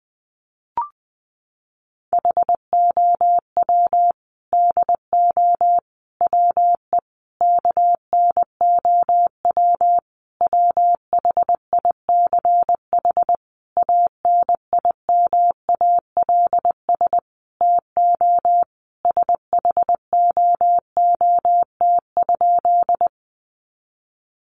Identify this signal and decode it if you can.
Morse code, heard as a keyed tone, about 20 wpm, 700 Hz. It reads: HOW DO WE KNOW WHICH ANIMALS TO SHOOT?
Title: A Sound Of Thunder 20wpm